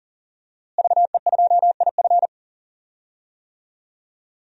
Morse code, heard as a keyed tone, about 40 wpm, 700 Hz. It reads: VE2IF
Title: DXCC-100 Call Signs - 5 Characters - Repeated 3x - 40wpm